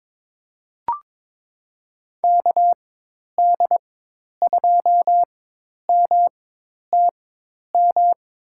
Morse code, heard as a keyed tone, about 22 wpm, spaced as for 10 wpm, 700 Hz. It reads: KD2MTM
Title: USA Call Signs - 6 Characters - 4x Character Spacing - 22wpm